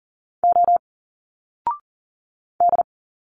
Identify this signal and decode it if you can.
Morse code, heard as a keyed tone, about 40 wpm, 700 Hz. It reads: O D
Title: Single Letters 40wpm